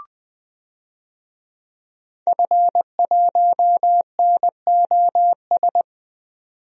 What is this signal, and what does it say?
Morse code, heard as a keyed tone, about 20 wpm, 700 Hz. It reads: F1NOS
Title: DXCC-100 Call Signs - 5 Characters - 20wpm